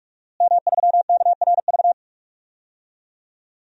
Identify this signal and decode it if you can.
Morse code, heard as a keyed tone, about 46 wpm, 700 Hz. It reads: M3KRV
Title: DXCC-100 Call Signs - 5 Characters - Repeated 2x - 45wpm